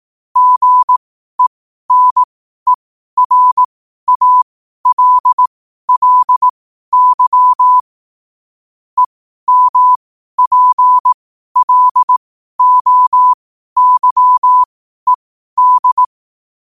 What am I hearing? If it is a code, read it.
Morse code, heard as a keyed tone, about 18 wpm, spaced as for 12 wpm, 1000 Hz. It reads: GENERALLY EMPLOYED
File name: qotdfiles / 12wpm